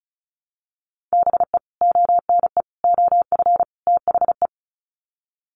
Morse code, heard as a keyed tone, about 34 wpm, 700 Hz. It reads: BE ONE OF THE
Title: Sets of 4 Words 35wpm